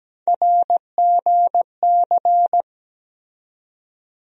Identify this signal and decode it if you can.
Morse code, heard as a keyed tone, about 17 wpm, 700 Hz. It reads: RGC
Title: USA Call Sign Postfixes - 3 Letters 17wpm